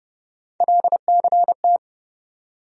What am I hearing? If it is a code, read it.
Morse code, heard as a keyed tone, about 30 wpm, 700 Hz. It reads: LCT
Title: USA Call Sign Postfixes - 3 Letters 30wpm